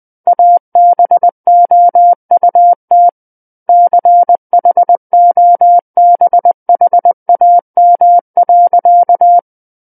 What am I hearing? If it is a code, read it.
Morse code, heard as a keyed tone, about 20 wpm, 700 Hz. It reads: ABOUT CHOBHAM.
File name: 11-Chapter 11 - 20 WPM